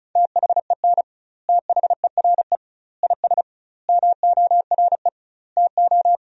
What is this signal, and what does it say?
Morse code, heard as a keyed tone, about 36 wpm, 700 Hz. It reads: THEN THERE IS MORE TO
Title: 2x Word-Spacing - Sentences from Top 100 Words - 35wpm